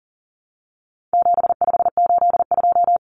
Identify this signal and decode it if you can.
Morse code, heard as a keyed tone, about 40 wpm, 700 Hz. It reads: 7582
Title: Numbers - 4-Digit 40wpm